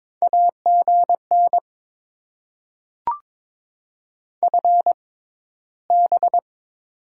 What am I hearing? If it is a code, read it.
Morse code, heard as a keyed tone, about 22 wpm, spaced as for 14 wpm, 700 Hz. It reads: AGN FB
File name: Top 20 QSO Elements - 6x Character Spacing - 22wpm